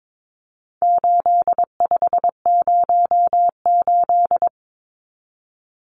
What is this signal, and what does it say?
Morse code, heard as a keyed tone, about 22 wpm, 700 Hz. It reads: 8508
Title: Numbers - 4-Digit 22wpm